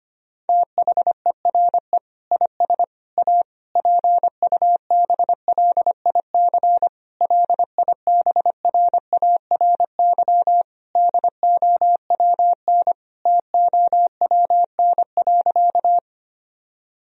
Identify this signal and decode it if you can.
Morse code, heard as a keyed tone, about 25 wpm, 700 Hz. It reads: THERE IS A PUBLIC LIBRARY DOWN TOWN.
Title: Sentences from Top 1000 Words - Review Part 4 - 25wpm